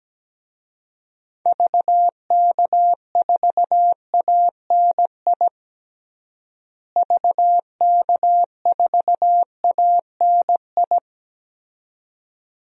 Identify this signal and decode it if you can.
Morse code, heard as a keyed tone, about 17 wpm, 700 Hz. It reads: VK4ANI VK4ANI
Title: DXCC-100 Call Signs - 6 Characters - Repeated 2x - 17wpm